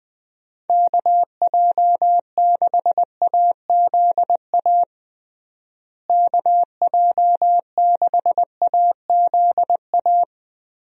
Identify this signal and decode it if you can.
Morse code, heard as a keyed tone, about 20 wpm, 700 Hz. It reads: KJ6AZA KJ6AZA
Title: USA Call Signs - 6 Characters - Repeated 3x - 20wpm